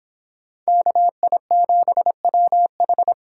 Morse code, heard as a keyed tone, about 26 wpm, 700 Hz. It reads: KI7WH